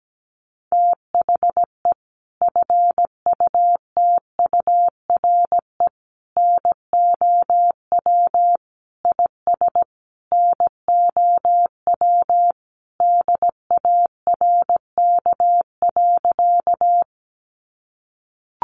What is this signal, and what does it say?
Morse code, heard as a keyed tone, about 17 wpm, 700 Hz. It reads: THE FUTURE NOW IS NOW DARK.
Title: Sentences from Top 700 Words 17wpm